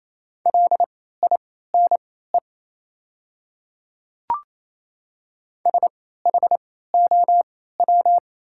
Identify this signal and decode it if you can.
Morse code, heard as a keyed tone, about 27 wpm, spaced as for 16 wpm, 690 Hz. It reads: LINE SHOW
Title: 3x Character Spacing - Top 200 Words - 28wpm